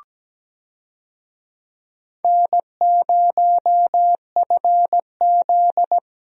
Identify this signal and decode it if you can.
Morse code, heard as a keyed tone, about 17 wpm, 700 Hz. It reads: N0FZ
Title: USA Call Signs - 4 Characters - Repeated 2x - 17wpm